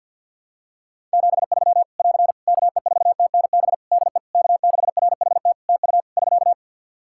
Morse code, heard as a keyed tone, about 51 wpm, 700 Hz. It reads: 73 <AR> KE4TNB DE K6RHT TU <SK>
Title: Practice QSO - Exchange 4 - Ending - 50wpm